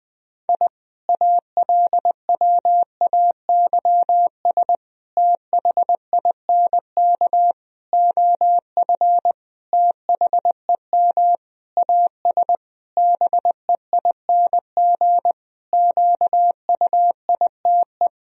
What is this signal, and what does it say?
Morse code, heard as a keyed tone, about 21 wpm, 700 Hz. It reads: I ALWAYS THINK OF THEM AS BEING QUITE
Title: Sentences from Top 1000 Words - Review Part 3 - 20wpm